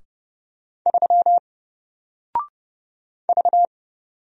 Morse code, heard as a keyed tone, about 30 wpm, 700 Hz. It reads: 3 V
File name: Single-Letter-Number-30wpm